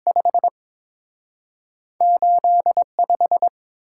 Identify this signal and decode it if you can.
Morse code, heard as a keyed tone, about 30 wpm, 700 Hz. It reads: 5 85